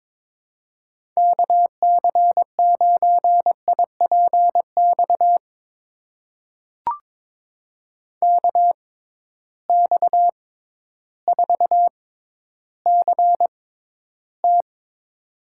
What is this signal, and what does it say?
Morse code, heard as a keyed tone, about 22 wpm, spaced as for 12 wpm, 700 Hz. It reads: KC9IPX KX4CT
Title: USA Call Signs - 6x Character Spacing - 22wpm